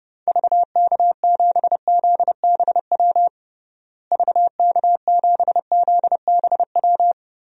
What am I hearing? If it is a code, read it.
Morse code, heard as a keyed tone, about 32 wpm, 700 Hz. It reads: VK7ZBW VK7ZBW
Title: DXCC-100 Call Signs - 6 Characters - Repeated 2x - 30wpm